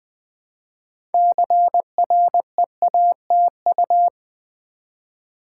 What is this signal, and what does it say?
Morse code, heard as a keyed tone, about 20 wpm, 700 Hz. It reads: CREATU
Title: Build A Word - 2 Letters At A Time - 8-Letter Words - 20wpm